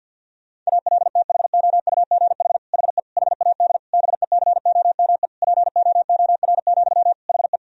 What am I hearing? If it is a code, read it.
Morse code, heard as a keyed tone, about 50 wpm, 700 Hz. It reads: ALTHOUGH HE HAD BECOME POOR, HE
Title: Sentences from Top 1000 Words - Review Part 3 - 50wpm